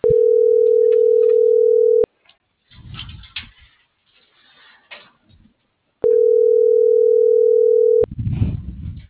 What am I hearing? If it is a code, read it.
Ringback tone (North America), 440 + 480 Hz.